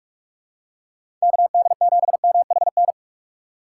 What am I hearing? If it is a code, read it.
Morse code, heard as a keyed tone, about 46 wpm, 700 Hz. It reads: KD7MHN
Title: USA Call Signs - 6 Characters - 2x Character Spacing - 45wpm